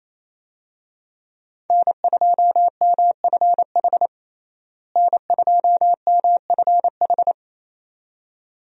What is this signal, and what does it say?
Morse code, heard as a keyed tone, about 27 wpm, 700 Hz. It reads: N2MFH N2MFH
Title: USA Call Signs - 5 Characters - Repeated 2x - 28wpm